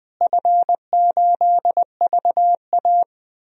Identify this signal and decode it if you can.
Morse code, heard as a keyed tone, about 20 wpm, 700 Hz. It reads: F8VA
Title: DXCC-100 Call Signs - 4 Characters - Repeated 2x - 20wpm